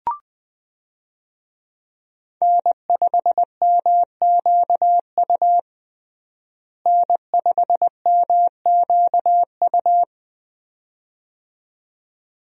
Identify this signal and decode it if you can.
Morse code, heard as a keyed tone, about 20 wpm, 700 Hz. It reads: N5MQU N5MQU
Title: USA Call Signs - 5 Characters - Repeated 2x - 20wpm